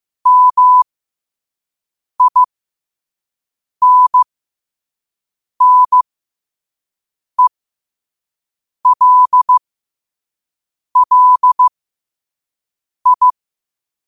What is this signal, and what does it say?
Morse code, heard as a keyed tone, about 15 wpm, spaced as for 5 wpm, 1000 Hz. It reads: MINNELLI
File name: qotdfiles / 5wpm